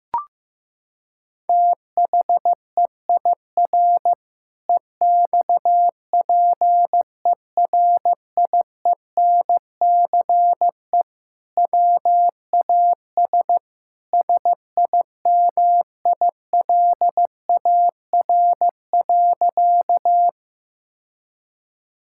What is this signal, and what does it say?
Morse code, heard as a keyed tone, about 15 wpm, 700 Hz. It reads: THEIR EXPERIENCE WAS SIMILAR.
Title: Sentences from Top 1000 Words - Review Part 5 - 15wpm